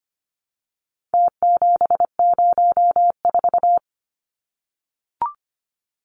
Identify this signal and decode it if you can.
Morse code, heard as a keyed tone, about 25 wpm, 700 Hz. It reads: T704
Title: Numbers - Contest Style - 4-Digit 25wpm